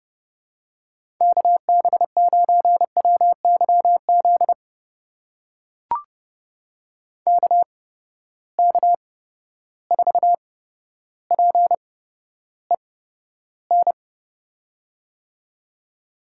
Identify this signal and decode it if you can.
Morse code, heard as a keyed tone, about 31 wpm, spaced as for 13 wpm, 700 Hz. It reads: KB9WYZ KK4PEN
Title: USA Call Signs - 6 Characters - 8x Character Spacing - 30wpm